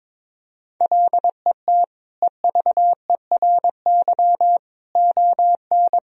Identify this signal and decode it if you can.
Morse code, heard as a keyed tone, about 22 wpm, 700 Hz. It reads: LET EVERY ON
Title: Sentences from Top 1000 Words - Review Part 1 - 22wpm